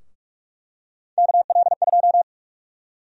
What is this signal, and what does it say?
Morse code, heard as a keyed tone, about 46 wpm, 700 Hz. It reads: KL2
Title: USA Call Sign Prefixes - 45wpm